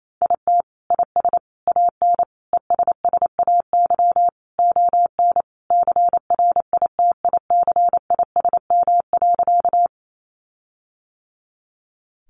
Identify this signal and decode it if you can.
Morse code, heard as a keyed tone, about 28 wpm, 700 Hz. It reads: IT IS AN ESSAY ON CRITICISM.
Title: Sentences from Top 1500 Words 28wpm